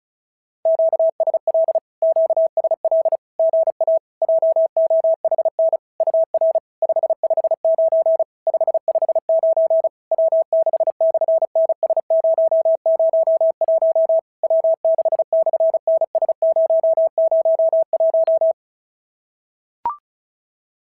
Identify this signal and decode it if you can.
Morse code, heard as a keyed tone, about 36 wpm, 650 Hz. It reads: QSL QSL GA JOHN UR 559 559 W6/NS001 W6/NS001
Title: SOTA QSO Practice 35wpm